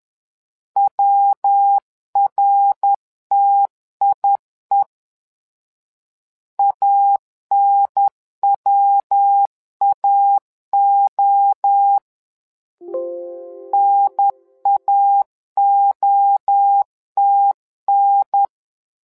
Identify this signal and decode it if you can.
Morse code, heard as a keyed tone, about 10 wpm, 800 Hz. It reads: WRTIE ANWAO NAOTN
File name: LEZIONI / 3 MWRO / 3 MWRO 10wpm 2148